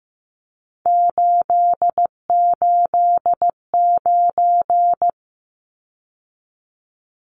Morse code, heard as a keyed tone, about 15 wpm, 700 Hz. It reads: 889
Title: Numbers - 3-Digit 15wpm